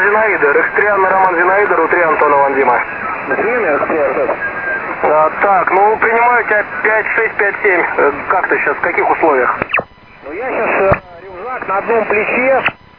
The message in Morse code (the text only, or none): none